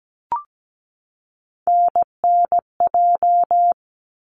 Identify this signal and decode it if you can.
Morse code, heard as a keyed tone, about 17 wpm, 700 Hz. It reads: NNJ
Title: ARRL Sections 17wpm